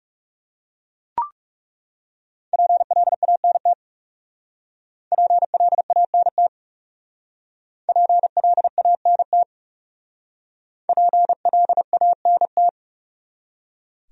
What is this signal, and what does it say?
Morse code, heard as a keyed tone, about 37 wpm, 700 Hz. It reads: PLANT PLANT PLANT PLANT
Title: Speed-Racing Top 300 Words 30wpm